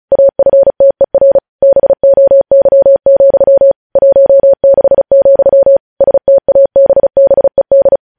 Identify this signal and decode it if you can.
Morse code, heard as a keyed tone, about 36 wpm, 550 Hz. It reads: AFTER BOY, 16, STABBED
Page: news headlines in morse code at 35 wpm